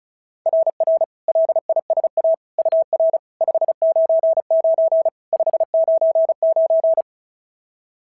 Morse code, heard as a keyed tone, about 36 wpm, 660 Hz. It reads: RR LISA UR 599 599
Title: SOTA QSO Practice 35wpm